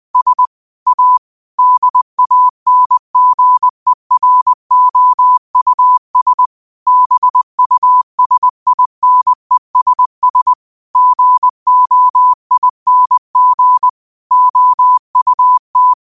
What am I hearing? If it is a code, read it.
Morse code, heard as a keyed tone, about 20 wpm, 1000 Hz. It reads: S A DANGEROUS BUSINESS GOING OUT
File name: qotdfiles / 20wpm